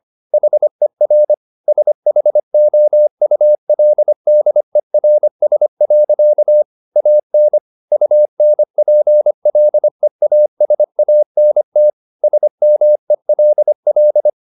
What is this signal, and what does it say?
Morse code, heard as a keyed tone, about 25 wpm, 600 Hz. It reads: HER SHOULDERS. AN UNPLEASANT SMELL